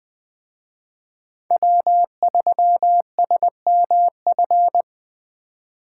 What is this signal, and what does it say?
Morse code, heard as a keyed tone, about 21 wpm, 700 Hz. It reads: W3SMF
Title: Speed Racing - USA Call Signs 17wpm